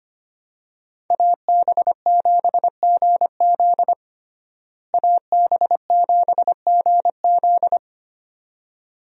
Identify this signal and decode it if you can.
Morse code, heard as a keyed tone, about 25 wpm, 700 Hz. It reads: AB7GZ AB7GZ
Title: USA Call Signs - 5 Characters - Repeated 2x - 25wpm